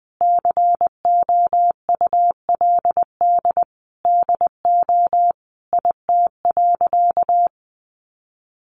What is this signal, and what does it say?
Morse code, heard as a keyed tone, about 20 wpm, 700 Hz. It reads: COULD DO IT.
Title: Sentences from Top 1100 Words 20wpm